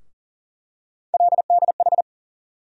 Morse code, heard as a keyed tone, about 40 wpm, 700 Hz. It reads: LDH
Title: USA Call Sign Postfixes - 3 Letters 40wpm